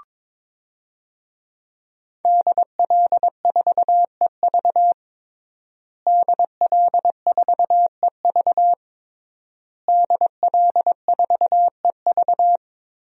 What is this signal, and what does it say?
Morse code, heard as a keyed tone, about 22 wpm, 700 Hz. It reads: DL4EV DL4EV DL4EV
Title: DXCC-100 Call Signs - 5 Characters - Repeated 3x - 22wpm